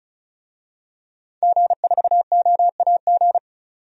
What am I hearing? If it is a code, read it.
Morse code, heard as a keyed tone, about 36 wpm, 700 Hz. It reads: G4OAG
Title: DXCC-100 Call Signs - 5 Characters - 35wpm